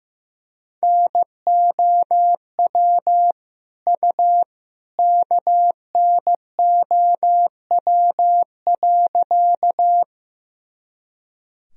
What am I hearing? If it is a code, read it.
Morse code, heard as a keyed tone, about 15 wpm, 700 Hz. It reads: NOW U KNOW.
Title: Sentences from Top 100 Words and Abbreviations 15wpm